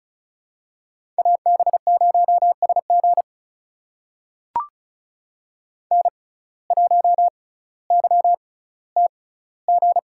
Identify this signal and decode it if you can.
Morse code, heard as a keyed tone, about 34 wpm, spaced as for 18 wpm, 700 Hz. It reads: AB0SG N1YTG
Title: USA Call Signs - 5 Characters - 6x Character Spacing - 35wpm